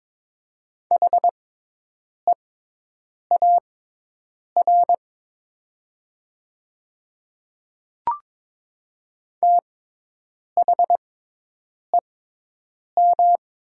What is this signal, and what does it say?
Morse code, heard as a keyed tone, about 22 wpm, spaced as for 8 wpm, 700 Hz. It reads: HEAR THEM
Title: 6x Character Spacing - Top 500 Words - 22wpm